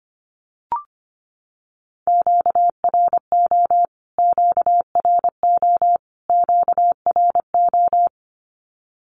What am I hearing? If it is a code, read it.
Morse code, heard as a keyed tone, about 25 wpm, 700 Hz. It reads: QRO QRO QRO
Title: Q-Codes Repeated 3x 25wpm